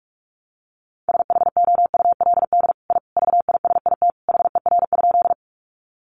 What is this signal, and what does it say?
Morse code, heard as a keyed tone, about 44 wpm, 700 Hz. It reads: SHOULD I VISIT HER?
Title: Sentences from Top 900 Words 45wpm